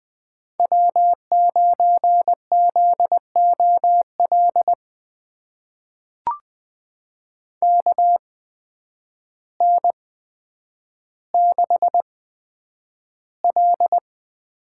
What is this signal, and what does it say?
Morse code, heard as a keyed tone, about 20 wpm, spaced as for 9 wpm, 700 Hz. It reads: W9ZOL KN6L
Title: USA Call Signs - 5 Characters - 8x Character Spacing - 20wpm